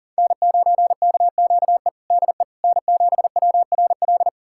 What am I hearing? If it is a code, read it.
Morse code, heard as a keyed tone, about 40 wpm, 700 Hz. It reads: N9KQE DE N7WRL
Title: USA Callsign DE Callsign - 5 Characters - 40wpm